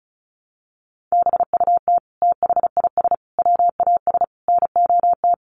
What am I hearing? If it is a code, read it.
Morse code, heard as a keyed tone, about 34 wpm, 700 Hz. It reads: BUT THIS WAS NOT
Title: Sentences from Top 600 Words 35wpm